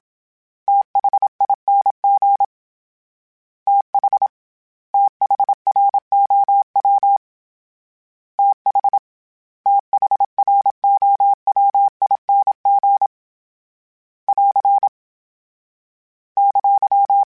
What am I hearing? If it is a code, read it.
Morse code, heard as a keyed tone, about 27 wpm, 800 Hz. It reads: THING TH THROW TH THROWING <AR> !
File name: THs / TH / TH Prefixes 1 @25